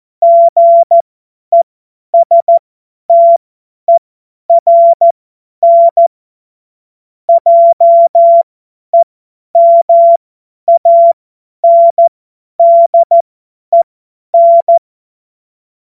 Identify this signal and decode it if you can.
Morse code, heard as a keyed tone, about 14 wpm, spaced as for 10 wpm, 680 Hz. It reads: GESTERN JEMANDEN
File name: Morsekurs / Witze / Witz 05. Oktober 2021, 50 F70